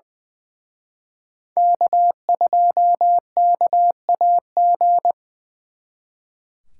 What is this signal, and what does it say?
Morse code, heard as a keyed tone, about 24 wpm, 700 Hz. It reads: K2KAG
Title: Speed Racing - USA 1x3 Call Signs 20wpm